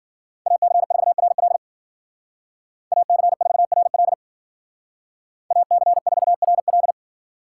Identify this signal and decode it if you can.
Morse code, heard as a keyed tone, about 49 wpm, 700 Hz. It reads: AC4RL AC4RL AC4RL
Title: Speed Racing - USA 2x2 Call Signs 40wpm